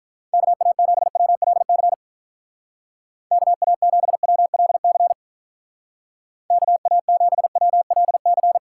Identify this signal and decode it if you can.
Morse code, heard as a keyed tone, about 44 wpm, 700 Hz. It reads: KA7WLC KA7WLC KA7WLC
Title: Speed Racing - USA 2x3 Call Signs 35wpm